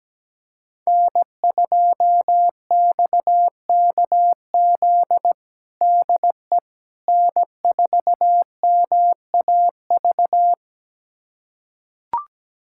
Morse code, heard as a keyed tone, about 17 wpm, 700 Hz. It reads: N2XKZ DE N4MAV
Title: USA Callsign DE Callsign - 5 Characters - 17wpm